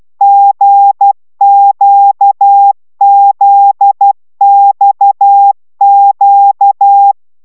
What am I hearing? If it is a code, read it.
Morse code, heard as a keyed tone, about 12 wpm, 800 Hz. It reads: GQZXQ